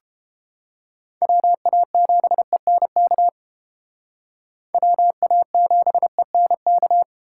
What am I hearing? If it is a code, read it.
Morse code, heard as a keyed tone, about 30 wpm, 700 Hz. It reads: WA7ENK WA7ENK